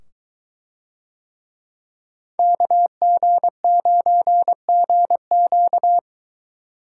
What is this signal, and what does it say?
Morse code, heard as a keyed tone, about 22 wpm, 700 Hz. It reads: KG9GQ